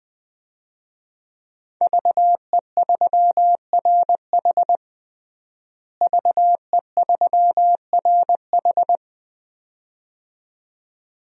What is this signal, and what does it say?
Morse code, heard as a keyed tone, about 20 wpm, 700 Hz. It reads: VE3RH VE3RH
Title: DXCC-100 Call Signs - 5 Characters - Repeated 2x - 20wpm